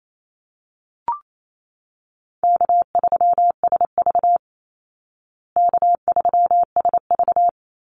Morse code, heard as a keyed tone, about 28 wpm, 700 Hz. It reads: K3SV K3SV
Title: Easy Call Signs Repeated 2x - 28wpm